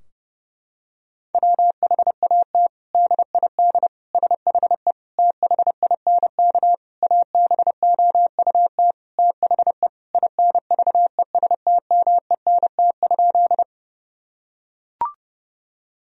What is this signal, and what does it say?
Morse code, heard as a keyed tone, about 30 wpm, 700 Hz. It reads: WHAT DID SHE THINK ABOUT THE INVESTMENT?
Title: Sentences from Top 1000 Words - Review Part 4 - 30wpm